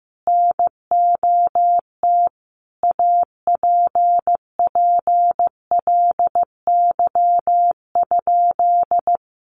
Morse code, heard as a keyed tone, about 15 wpm, 700 Hz. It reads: NOT APPLY?
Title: Sentences from Top 1200 Words 15wpm